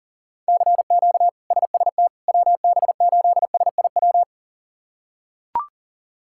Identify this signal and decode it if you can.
Morse code, heard as a keyed tone, about 40 wpm, 700 Hz. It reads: CQ SST WB8SIW
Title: K1USN SST QSO Practice - 40wpm